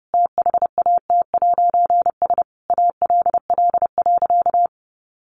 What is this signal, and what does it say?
Morse code, heard as a keyed tone, about 28 wpm, 700 Hz. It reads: THAT'S ALL.
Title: A Sound Of Thunder 30wpm